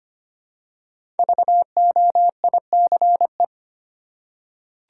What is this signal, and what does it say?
Morse code, heard as a keyed tone, about 25 wpm, 700 Hz. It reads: VOICE
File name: 5 Letter Words - First 3 Letters Given - 25wpm